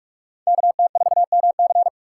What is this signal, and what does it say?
Morse code, heard as a keyed tone, about 47 wpm, 700 Hz. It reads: KT4MC
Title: USA Call Signs - 5 Characters - 4x Character Spacing - 45wpm